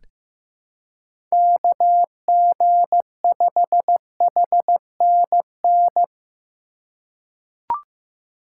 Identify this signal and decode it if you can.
Morse code, heard as a keyed tone, about 15 wpm, 700 Hz. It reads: KG5HNN
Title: USA Call Signs - 4x Character Spacing - 15wpm